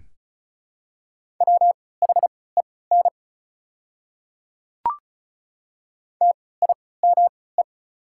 Morse code, heard as a keyed tone, about 34 wpm, spaced as for 20 wpm, 690 Hz. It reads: WHEN TIME
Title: 3x Character Spacing - Top 100 Words - 35wpm